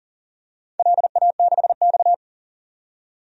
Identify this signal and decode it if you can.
Morse code, heard as a keyed tone, about 40 wpm, 700 Hz. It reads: LA6X
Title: DXCC-100 Call Signs - 4 Characters - 40wpm